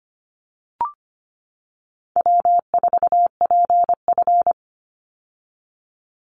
Morse code, heard as a keyed tone, about 25 wpm, 700 Hz. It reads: W4PF
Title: Easy Call Signs - 25wpm